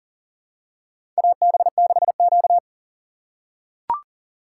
Morse code, heard as a keyed tone, about 40 wpm, 700 Hz. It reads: AB6Q
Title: USA Call Signs - 4 Characters - 4x Character Spacing - 40wpm